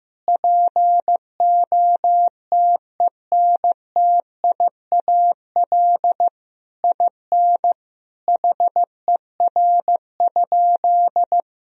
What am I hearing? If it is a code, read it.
Morse code, heard as a keyed tone, about 15 wpm, 700 Hz. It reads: POTENTIAL IN HER?
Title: Sentences from Top 1000 Words - Review Part 4 - 15wpm